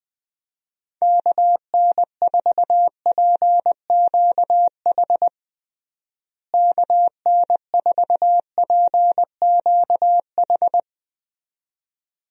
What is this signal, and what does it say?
Morse code, heard as a keyed tone, about 20 wpm, 700 Hz. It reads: KN4PQH KN4PQH
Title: USA Call Signs - 6 Characters - Repeated 2x - 20wpm